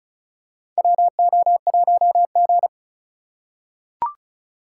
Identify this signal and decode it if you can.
Morse code, heard as a keyed tone, about 36 wpm, 700 Hz. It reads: WO1G
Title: USA Call Signs - 4 Characters - 3x Character Spacing - 35wpm